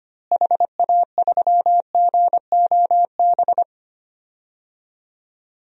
Morse code, heard as a keyed tone, about 25 wpm, 700 Hz. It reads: HA3GOB
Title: DXCC-100 Call Signs - 6 Characters - Repeated 3x - 25wpm